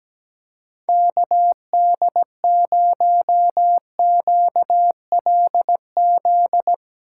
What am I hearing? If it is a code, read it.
Morse code, heard as a keyed tone, about 19 wpm, 700 Hz. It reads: KD0QLZ
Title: Speed Racing - USA 2x3 Call Signs 17wpm